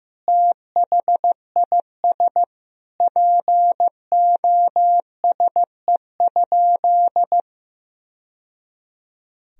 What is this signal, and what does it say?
Morse code, heard as a keyed tone, about 15 wpm, 700 Hz. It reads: THIS POSE?
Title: Sentences from Top 1000 Words - Review Part 2 - 15wpm